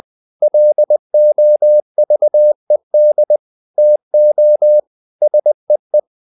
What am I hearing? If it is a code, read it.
Morse code, heard as a keyed tone, about 20 wpm, 600 Hz. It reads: LOVED TO SEE
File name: the old man and the sea / the old man and the sea part 1 20wpm